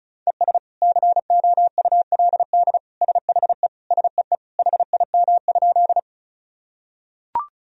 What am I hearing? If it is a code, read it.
Morse code, heard as a keyed tone, about 34 wpm, 700 Hz. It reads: ES COULD SHE SEE HIM?
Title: Sentences from Top 100 Words and Abbreviations 35wpm